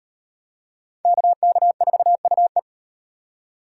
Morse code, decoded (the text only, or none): KK4UE